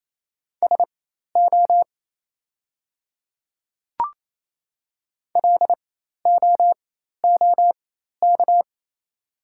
Morse code, heard as a keyed tone, about 28 wpm, spaced as for 13 wpm, 700 Hz. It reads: SO LOOK
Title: 4x Character Spacing - Top 200 Words - 28wpm